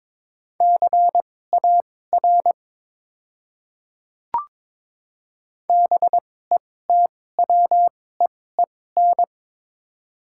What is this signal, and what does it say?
Morse code, heard as a keyed tone, about 22 wpm, spaced as for 16 wpm, 700 Hz. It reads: CAR BETWEEN
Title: 2x Character Spacing - Top 300 Words - 22wpm